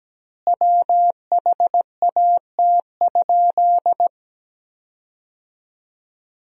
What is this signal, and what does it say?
Morse code, heard as a keyed tone, about 17 wpm, 700 Hz. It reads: WHAT?
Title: 2x Word-Spacing - Sentences from Top 300 Words - 17wpm